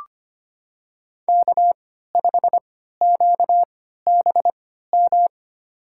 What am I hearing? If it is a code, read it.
Morse code, heard as a keyed tone, about 25 wpm, spaced as for 14 wpm, 700 Hz. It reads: K5QBM
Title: USA Call Signs - 5 Characters - 3x Character Spacing - 25wpm